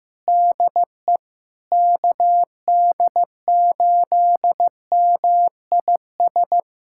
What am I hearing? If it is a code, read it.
Morse code, heard as a keyed tone, about 15 wpm, 700 Hz. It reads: DE KD8MIS
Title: USA Callsign DE Callsign 15wpm